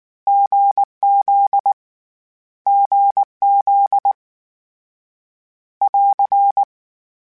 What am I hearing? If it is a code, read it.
Morse code, heard as a keyed tone, about 19 wpm, 800 Hz. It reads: GZ GZ <AR>
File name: CW Code Groups-mp3s / Sets mp3s / GZ GZ7 sp @18